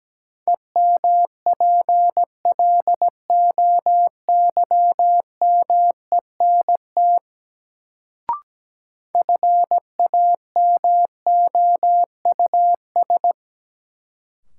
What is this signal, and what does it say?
Morse code, heard as a keyed tone, about 17 wpm, 700 Hz. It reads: EMPLOYMENT FAMOUS
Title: Sentences from Top 1500 Words - New Word Drill - 17wpm